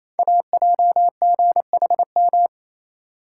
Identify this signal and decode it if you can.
Morse code, heard as a keyed tone, about 28 wpm, 700 Hz. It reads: AJGHM
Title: Code Groups - 5 Letters - 28wpm